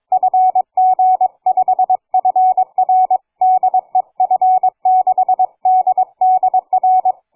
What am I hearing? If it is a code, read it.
Morse code, heard as a keyed tone, about 22 wpm, 740 Hz. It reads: FG5FR DE F6DDR
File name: cw forum / fg5fr 160m 02h30